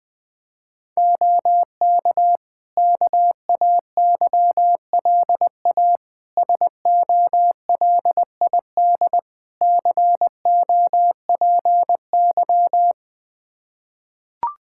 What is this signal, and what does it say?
Morse code, heard as a keyed tone, about 20 wpm, 700 Hz. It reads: OK KAYLA SOLID COPY
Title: K1USN SST QSO Practice 20wpm